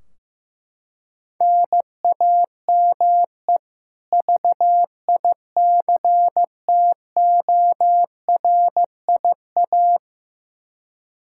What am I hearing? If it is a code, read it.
Morse code, heard as a keyed tone, about 15 wpm, 700 Hz. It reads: NAME VICTORIA
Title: Practice QSO - Exchange 1 - Beginning - 15wpm